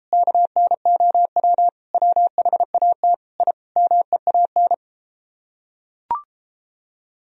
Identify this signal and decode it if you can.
Morse code, heard as a keyed tone, about 33 wpm, 700 Hz. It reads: KNOW WHAT I MEAN